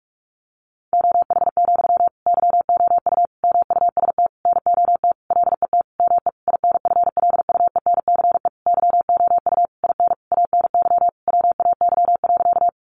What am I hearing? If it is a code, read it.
Morse code, heard as a keyed tone, about 44 wpm, 700 Hz. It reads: OH, YOU MUST NOT LET ME INFLUENCE YOU IN ANY WAY.
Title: Sentences from Top 1100 Words 45wpm